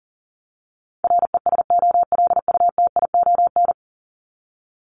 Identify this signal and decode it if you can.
Morse code, heard as a keyed tone, about 40 wpm, 700 Hz. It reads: RESOLUTION
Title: Top 1501-1750 Words 40wpm